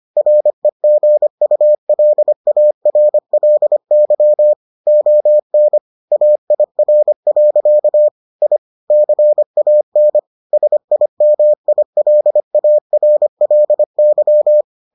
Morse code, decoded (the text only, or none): REGULARLY ON AIR. I CAN SIMILARLY